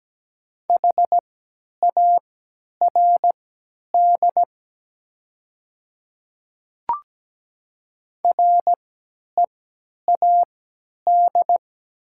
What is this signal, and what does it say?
Morse code, heard as a keyed tone, about 17 wpm, spaced as for 10 wpm, 700 Hz. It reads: HARD READ
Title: 3x Character Spacing - Top 500 Words - 17wpm